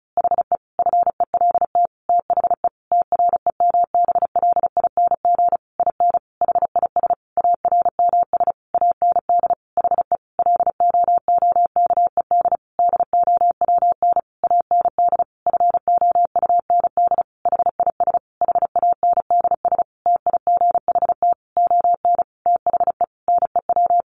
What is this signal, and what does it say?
Morse code, heard as a keyed tone, about 34 wpm, 700 Hz. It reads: HE FELT THE TREMBLING IN HIS ARMS AND HE LOOKED DOWN AND FOUND HIS HANDS TIGHT ON THE NEW
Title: A Sound Of Thunder 35wpm